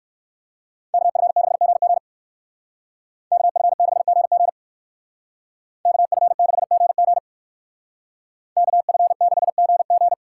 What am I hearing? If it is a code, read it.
Morse code, heard as a keyed tone, about 56 wpm, 700 Hz. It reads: KF6GG KF6GG KF6GG KF6GG
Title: Speed Racing - USA Call Signs 45wpm